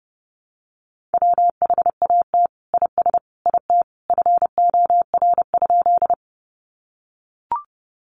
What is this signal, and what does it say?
Morse code, heard as a keyed tone, about 30 wpm, 700 Hz. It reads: WHAT IS IT FOR?
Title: Sentences from Top 100 Words 30wpm